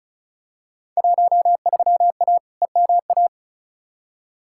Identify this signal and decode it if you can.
Morse code, heard as a keyed tone, about 34 wpm, 700 Hz. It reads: 13A EMA
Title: Field Day Exchanges Encore - 35wpm